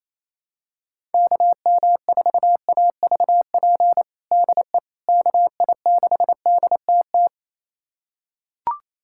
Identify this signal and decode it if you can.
Morse code, heard as a keyed tone, about 28 wpm, 700 Hz. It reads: KM4AVP DE KI6DTT
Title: USA Callsign DE Callsign - 6 Characters - 28wpm